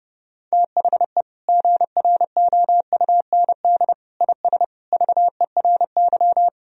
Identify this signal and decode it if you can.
Morse code, heard as a keyed tone, about 30 wpm, 700 Hz. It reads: THE GROUND IS VERY
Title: Sentences from Top 1000 Words - Review Part 2 - 30wpm